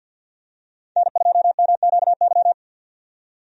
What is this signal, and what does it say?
Morse code, heard as a keyed tone, about 47 wpm, 700 Hz. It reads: N2MQY
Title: USA Call Signs - 5 Characters - 50wpm